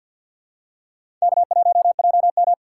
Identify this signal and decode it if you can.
Morse code, heard as a keyed tone, about 53 wpm, 700 Hz. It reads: K1JM
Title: USA Call Signs - 4 Characters - 2x Character Spacing - 50wpm